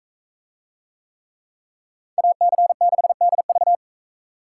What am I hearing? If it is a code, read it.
Morse code, heard as a keyed tone, about 42 wpm, 700 Hz. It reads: AC6DV